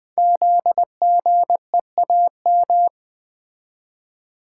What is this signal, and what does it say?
Morse code, heard as a keyed tone, about 20 wpm, 700 Hz. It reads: ZGEAM
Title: Code Groups - 5 Letters - 20wpm